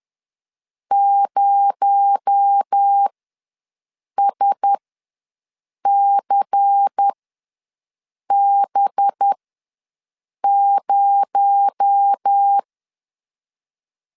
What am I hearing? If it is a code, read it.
Morse code, heard as a keyed tone, about 10 wpm, spaced as for 6 wpm, 800 Hz. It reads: 0SCB0
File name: LEZIONI / 6 PKX0 / 6 PKX0 ws 10wpm 4220